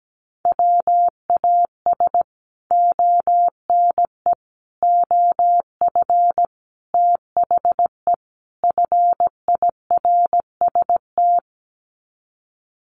Morse code, heard as a keyed tone, about 17 wpm, 700 Hz. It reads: WAS ONE OF THE FIRST
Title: Sets of 5 Words 17wpm